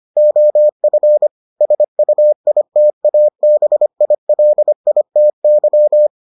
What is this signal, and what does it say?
Morse code, heard as a keyed tone, about 25 wpm, 600 Hz. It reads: OF SUITABILITY